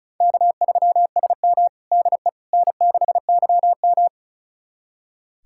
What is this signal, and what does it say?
Morse code, heard as a keyed tone, about 36 wpm, 700 Hz. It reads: K3SM DE N6YM
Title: USA Callsign DE Callsign - 4 Characters - 35wpm